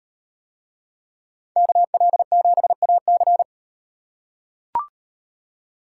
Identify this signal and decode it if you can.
Morse code, heard as a keyed tone, about 36 wpm, 700 Hz. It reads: KL7AC